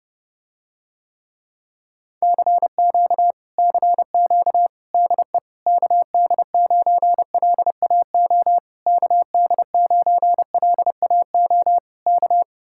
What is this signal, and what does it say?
Morse code, heard as a keyed tone, about 31 wpm, 700 Hz. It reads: CQ CQ DE KD9LAO KD9LAO K